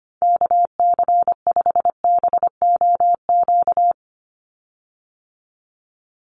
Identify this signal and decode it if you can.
Morse code, heard as a keyed tone, about 25 wpm, 690 Hz. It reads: KC5BOQ
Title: USA Call Signs 25wpm